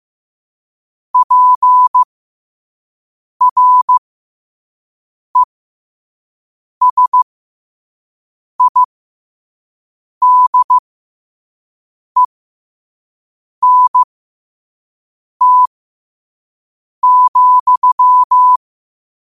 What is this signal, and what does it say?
Morse code, heard as a keyed tone, about 15 wpm, spaced as for 5 wpm, 1000 Hz. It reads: PRESIDENT,
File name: qotdfiles / 5wpm